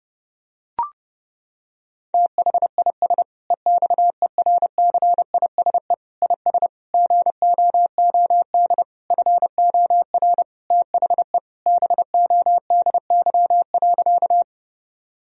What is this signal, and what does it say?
Morse code, heard as a keyed tone, about 32 wpm, 700 Hz. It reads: THIS EXERCISE IS GOOD FOR THE BODY.
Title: Sentences from Top 1000 Words - Review Part 4 - 30wpm